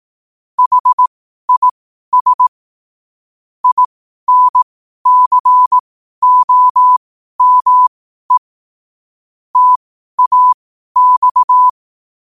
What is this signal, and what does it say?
Morse code, heard as a keyed tone, about 18 wpm, spaced as for 12 wpm, 1000 Hz. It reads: HIS INCOME TAX
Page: qotd-at-12-wpm